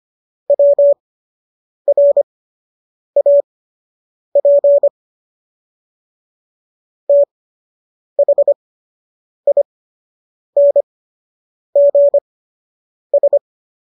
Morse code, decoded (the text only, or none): WRAP THINGS